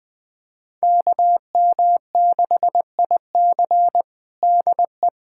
Morse code, heard as a keyed tone, about 20 wpm, 700 Hz. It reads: KM6IC DE
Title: USA Callsign DE Callsign - 5 Characters - 20wpm